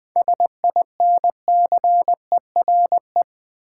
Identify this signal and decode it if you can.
Morse code, heard as a keyed tone, about 20 wpm, 700 Hz. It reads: SINCERE
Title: 7 Letter Words - First 3 Letters Given - 20wpm